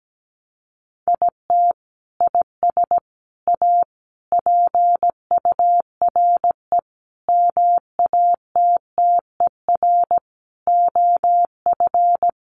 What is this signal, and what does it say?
Morse code, heard as a keyed tone, about 17 wpm, 700 Hz. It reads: IT IS A PURE MATTER OF
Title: Sentences from Top 1500 Words 17wpm